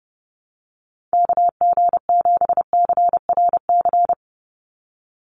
Morse code, heard as a keyed tone, about 30 wpm, 700 Hz. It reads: KG7CRC
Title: USA Call Signs 30wpm